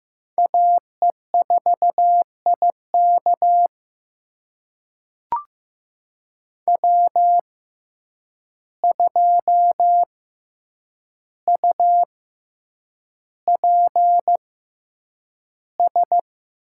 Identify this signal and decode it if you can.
Morse code, heard as a keyed tone, about 15 wpm, spaced as for 8 wpm, 700 Hz. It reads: AE4IK W2UPS
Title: USA Call Signs - 6x Character Spacing - 15wpm